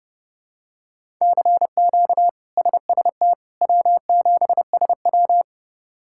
Morse code, heard as a keyed tone, about 30 wpm, 700 Hz. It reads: CQ SST W7SW
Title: K1USN SST QSO Practice - 30wpm